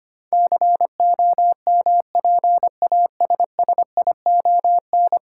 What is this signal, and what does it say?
Morse code, heard as a keyed tone, about 25 wpm, 700 Hz. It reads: COMPASSION
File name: 10 Letter Words - First 3 Letters Given - 25wpm